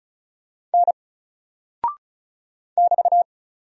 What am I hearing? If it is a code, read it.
Morse code, heard as a keyed tone, about 36 wpm, 700 Hz. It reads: N <BT>
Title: Single Character (Letter:Number:Punctuation:Prosign) 35wpm